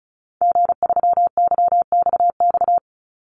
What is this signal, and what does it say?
Morse code, heard as a keyed tone, about 34 wpm, 700 Hz. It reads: G3YXX
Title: Call Signs - 35wpm